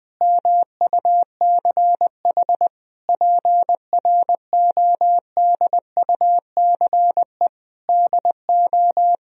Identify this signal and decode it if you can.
Morse code, heard as a keyed tone, about 21 wpm, 700 Hz. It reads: MUCH PRODUCE DO
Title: Sentences from Top 1000 Words - Review Part 3 - 20wpm